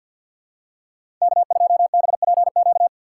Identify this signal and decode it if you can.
Morse code, heard as a keyed tone, about 52 wpm, 700 Hz. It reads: K2BPQ
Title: USA Call Signs - 5 Characters - 6x Character Spacing - 50wpm